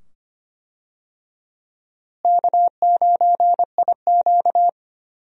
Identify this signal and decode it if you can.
Morse code, heard as a keyed tone, about 25 wpm, 700 Hz. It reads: K9IQ
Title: USA Call Signs - 4 Characters - Repeated 2x - 25wpm